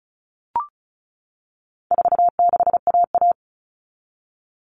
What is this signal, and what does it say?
Morse code, heard as a keyed tone, about 34 wpm, 700 Hz. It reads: 46AA
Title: All Cut Numbers - 4-Digit 35wpm